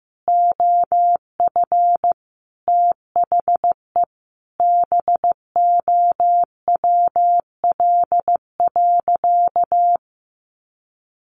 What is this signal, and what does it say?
Morse code, heard as a keyed tone, about 14 wpm, 700 Hz. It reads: OF THE BOWL.
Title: Sentences from Top 1200 Words 15wpm